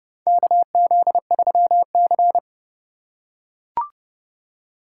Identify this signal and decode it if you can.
Morse code, heard as a keyed tone, about 31 wpm, 700 Hz. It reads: KZ3C
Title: USA Call Signs - 2x Character Spacing - 30wpm